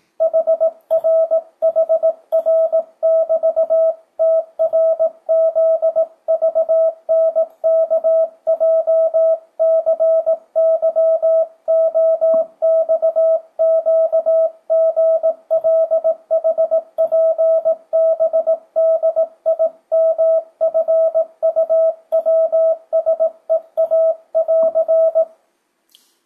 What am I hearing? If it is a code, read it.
Morse code, heard as a keyed tone, about 18 wpm, 650 Hz. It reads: HRHR<BT>TRZVNKJCYOXQGLHPBDIMFUWSEA<AR>
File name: 2025cw / speedup / 18wpm